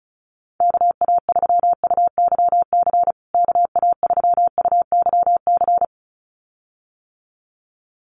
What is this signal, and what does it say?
Morse code, heard as a keyed tone, about 34 wpm, 700 Hz. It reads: KA3UYC KA3UYC
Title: USA Call Signs Repeated 2x 35wpm